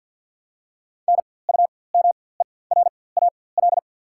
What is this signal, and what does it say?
Morse code, heard as a keyed tone, about 54 wpm, spaced as for 24 wpm, 700 Hz. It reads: NUMERAL
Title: 4x Character Spacing - Top 500 Words - 50wpm